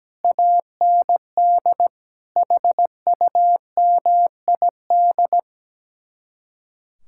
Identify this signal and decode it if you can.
Morse code, heard as a keyed tone, about 17 wpm, 700 Hz. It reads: AND HUMID
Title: Speed-Racing Common QSO Phrases - 17wpm